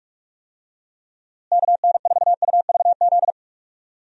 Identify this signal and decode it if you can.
Morse code, heard as a keyed tone, about 46 wpm, 700 Hz. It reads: KN4UVZ
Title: USA Call Signs - 6 Characters - Repeated 2x - 45wpm